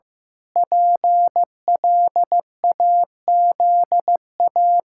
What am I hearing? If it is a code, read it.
Morse code, heard as a keyed tone, about 15 wpm, 700 Hz. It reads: PLAZA
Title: Pangram Challenge v2 - 15wpm